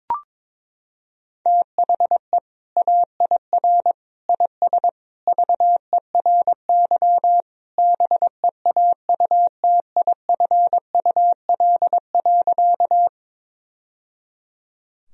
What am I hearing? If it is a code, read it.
Morse code, heard as a keyed tone, about 22 wpm, 700 Hz. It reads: THE AIR IS VERY BEAUTIFUL.
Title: Sentences from Top 1000 Words - Review Part 1 - 22wpm